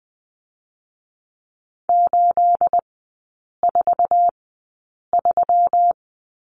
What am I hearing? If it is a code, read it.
Morse code, heard as a keyed tone, about 20 wpm, spaced as for 8 wpm, 690 Hz. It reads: 843
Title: 212 Lesson 34e - Number 8 - Three Characters 20wpm